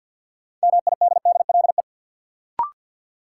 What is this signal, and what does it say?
Morse code, heard as a keyed tone, about 51 wpm, 700 Hz. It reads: MIDDLE
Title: Sentences from Top 900 Words - New Word Drill - 50wpm